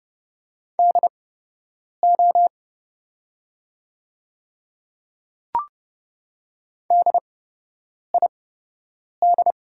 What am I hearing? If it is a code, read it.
Morse code, heard as a keyed tone, about 30 wpm, spaced as for 8 wpm, 700 Hz. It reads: DO DID
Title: 8x Character Spacing - Top 300 Words - 30wpm